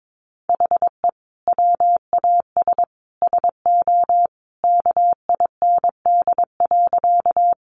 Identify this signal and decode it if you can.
Morse code, heard as a keyed tone, about 22 wpm, 700 Hz. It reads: HE WAS SO KIND.
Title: Sentences from Top 200 Words 22wpm